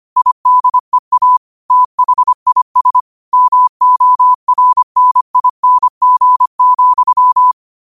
Morse code, heard as a keyed tone, about 25 wpm, 1000 Hz. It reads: IDEA THIS MORNING,
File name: qotdfiles / 25wpm